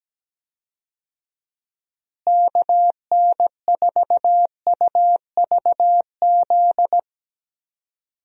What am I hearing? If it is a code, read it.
Morse code, heard as a keyed tone, about 17 wpm, 700 Hz. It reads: KN4UVZ
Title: USA Call Signs - 6 Characters - Repeated 2x - 17wpm